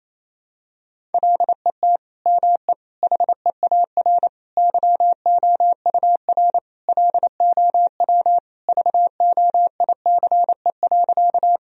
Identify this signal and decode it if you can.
Morse code, heard as a keyed tone, about 28 wpm, 700 Hz. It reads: LET ME HEAR YOUR LOW VOICE.
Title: Sentences from Top 1000 Words - Review Part 1 - 28wpm